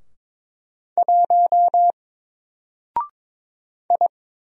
Morse code, heard as a keyed tone, about 22 wpm, 700 Hz. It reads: 1 I
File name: Single-Letter-Number-22wpm